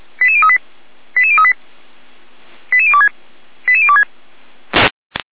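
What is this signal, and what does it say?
Five-tone selective call (ZVEI), heard as quick runs of five tones, tones of about 70 ms. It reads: 88038 88038 88026 88026
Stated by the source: Received by SDR; ZVEI five-tone call.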